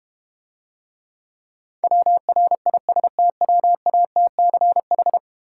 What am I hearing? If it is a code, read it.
Morse code, heard as a keyed tone, about 32 wpm, 700 Hz. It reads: WRISTWATCH